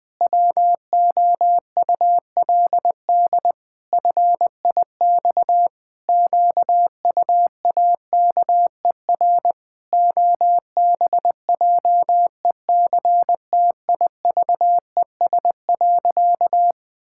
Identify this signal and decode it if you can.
Morse code, heard as a keyed tone, about 20 wpm, 700 Hz. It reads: WOULD FIX QUAKER OBJECTIVES.
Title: Pangram Challenge v2 - 20wpm